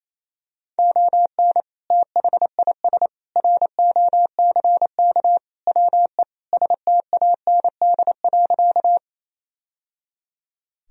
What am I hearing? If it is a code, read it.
Morse code, heard as a keyed tone, about 28 wpm, 700 Hz. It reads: ON THIS ROCK WE STAND.
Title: Sentences from Top 1000 Words - Review Part 2 - 28wpm